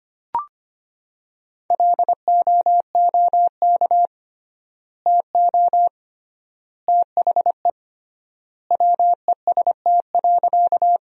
Morse code, heard as a keyed tone, about 25 wpm, 700 Hz. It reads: LOOK TO THE WEST.
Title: 3x Word-Spacing - Sentences from Top 400 Words - 25wpm